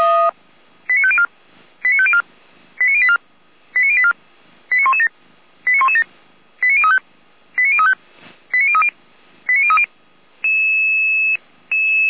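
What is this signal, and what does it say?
Five-tone selective call (ZVEI), heard as quick runs of five tones, tones of about 70 ms. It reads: 89504 89504 89084 89084 89117 89117 89035 89035 89030 89030
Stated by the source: ZVEI five-tone selective call 89080 received on SDR.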